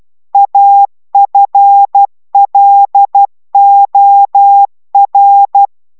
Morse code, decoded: AFLOR

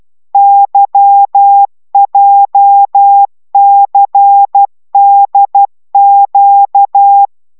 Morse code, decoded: YJCDQ

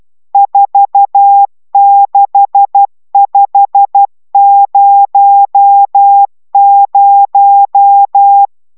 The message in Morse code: 46500

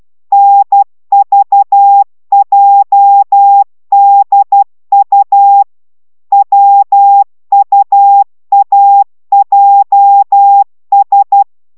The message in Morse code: NVJDU WUAJS

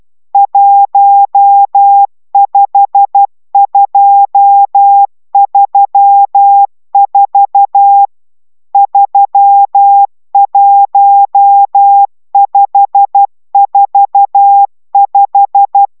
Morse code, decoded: 15234 31545

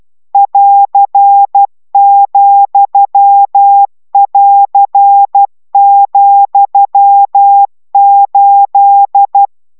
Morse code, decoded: <AR>,<AR>,8

